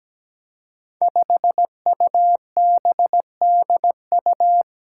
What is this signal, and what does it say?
Morse code, heard as a keyed tone, about 17 wpm, 700 Hz. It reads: 5UBDU
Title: Five Challenging Letter-Number Combinations 17wpm